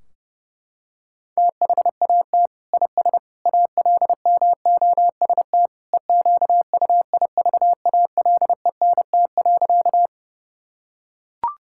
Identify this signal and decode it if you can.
Morse code, heard as a keyed tone, about 30 wpm, 700 Hz. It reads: THAT IS ALMOST EQUIVALENT.
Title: Sentences from Top 1600 Words 30wpm